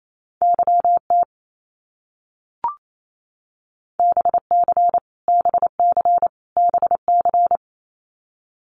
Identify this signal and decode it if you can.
Morse code, heard as a keyed tone, about 28 wpm, 700 Hz. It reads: YT BC BC BC
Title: Canadian Provinces and Territories - Repeated 3x - 28wpm